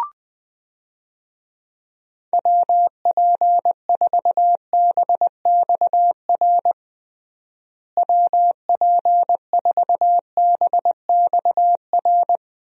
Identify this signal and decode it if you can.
Morse code, heard as a keyed tone, about 20 wpm, 700 Hz. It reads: WP4BXR WP4BXR
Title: USA Call Signs - 6 Characters - Repeated 2x - 20wpm